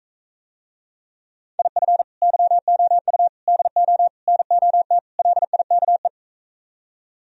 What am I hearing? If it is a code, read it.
Morse code, heard as a keyed tone, about 46 wpm, 700 Hz. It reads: IF YOU DO NOT LIKE